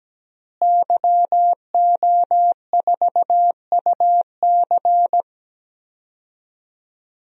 Morse code, heard as a keyed tone, about 17 wpm, 700 Hz. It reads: YO4UC
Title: DXCC-100 Call Signs - 5 Characters - Repeated 2x - 17wpm